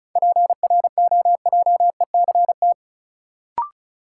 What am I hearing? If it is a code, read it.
Morse code, heard as a keyed tone, about 36 wpm, 700 Hz. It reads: PROJECT
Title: Sentences from Top 1400 Words - New Word Drill - 35wpm